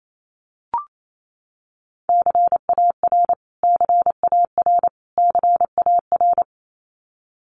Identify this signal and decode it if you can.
Morse code, heard as a keyed tone, about 28 wpm, 700 Hz. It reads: CAR CAR CAR
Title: Top 500 Words Repeated 28wpm